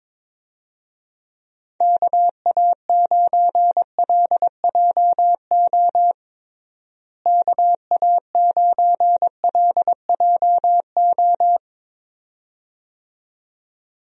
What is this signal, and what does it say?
Morse code, heard as a keyed tone, about 22 wpm, 700 Hz. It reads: KA9LJO KA9LJO
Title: USA Call Signs - 6 Characters - Repeated 2x - 22wpm